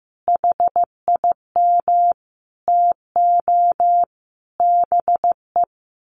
Morse code, heard as a keyed tone, about 14 wpm, 700 Hz. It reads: HIM TO BE
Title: Sentences from Top 1300 Words 15wpm